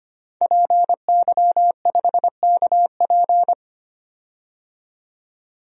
Morse code, heard as a keyed tone, about 25 wpm, 700 Hz. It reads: PY5KP
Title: DXCC-100 Call Signs - 5 Characters - 25wpm